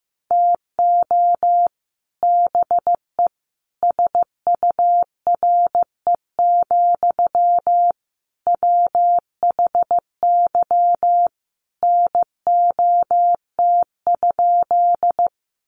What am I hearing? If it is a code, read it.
Morse code, heard as a keyed tone, about 15 wpm, 700 Hz. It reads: TO BE SURE, WHY NOT?
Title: Sentences from Top 600 Words 15wpm